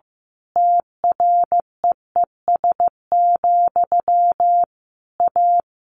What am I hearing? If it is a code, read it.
Morse code, heard as a keyed tone, about 15 wpm, 700 Hz. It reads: TREES, A
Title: A Sound Of Thunder 15wpm